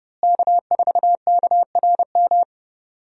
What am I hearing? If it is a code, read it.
Morse code, heard as a keyed tone, about 31 wpm, 700 Hz. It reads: K4KRM
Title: USA Call Signs - 5 Characters - Repeated 3x - 30wpm